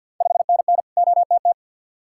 Morse code, heard as a keyed tone, about 45 wpm, 700 Hz. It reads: 5NN OTT
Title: Ontario QSO Party Exchange 50wpm